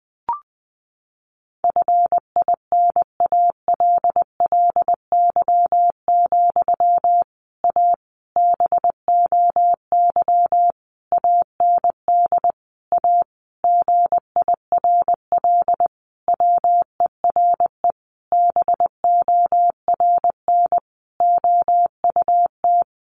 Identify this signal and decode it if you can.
Morse code, heard as a keyed tone, about 20 wpm, 700 Hz. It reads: FINALLY, A BOY AND A GIRL WERE BORN OUT
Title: Sentences from Top 1200 Words 20wpm